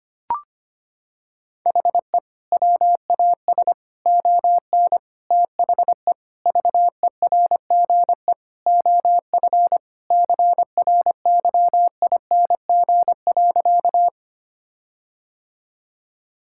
Morse code, heard as a keyed tone, about 25 wpm, 700 Hz. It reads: HE WAS ON THE VERGE OF CRYING.
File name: practice / 25wpm / Sentences with 4-Word Phrases - 25wpm